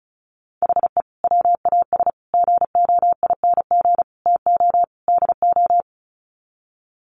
Morse code, heard as a keyed tone, about 34 wpm, 700 Hz. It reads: HE WAS GOING TO DO
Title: Sets of 5 Words Encore2 35wpm